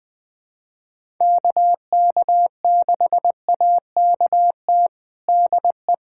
Morse code, heard as a keyed tone, about 21 wpm, 700 Hz. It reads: KK6AKT DE
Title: USA Callsign DE Callsign - 6 Characters - 20wpm